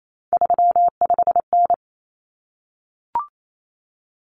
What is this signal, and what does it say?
Morse code, heard as a keyed tone, about 28 wpm, 700 Hz. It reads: 35N
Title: Common Cut Numbers - 3-Digit 28wpm